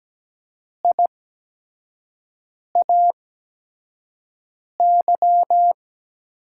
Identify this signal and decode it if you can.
Morse code, heard as a keyed tone, about 17 wpm, spaced as for 5 wpm, 700 Hz. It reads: IAY
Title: USA Call Signs - 6 Characters - 8x Character Spacing - 17wpm